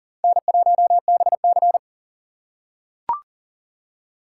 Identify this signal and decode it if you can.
Morse code, heard as a keyed tone, about 40 wpm, 700 Hz. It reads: N1BC
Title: USA Call Signs - 4 Characters - 4x Character Spacing - 40wpm